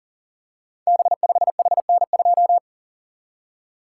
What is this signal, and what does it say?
Morse code, heard as a keyed tone, about 39 wpm, 700 Hz. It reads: B5HN2
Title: Code Groups - 5 Letter:Numbers - 40wpm